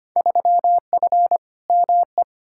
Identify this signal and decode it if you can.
Morse code, heard as a keyed tone, about 25 wpm, 700 Hz. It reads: 3F ME
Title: Field Day Exchanges - 25wpm